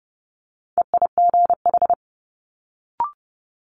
Morse code, heard as a keyed tone, about 30 wpm, 700 Hz. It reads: EIGH
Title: Multi-letter Phonemes Quick Fire 30wpm